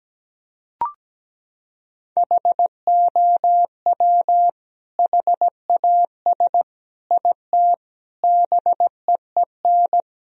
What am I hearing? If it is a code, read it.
Morse code, heard as a keyed tone, about 17 wpm, 700 Hz. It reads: HOW HAS IT BEEN
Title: Sentences from Top 1000 Words - Review Part 1 - 17wpm